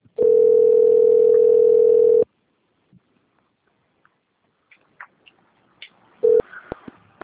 Ringback tone (North America), 440 + 480 Hz.
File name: rg-1600-unknown- - - .9771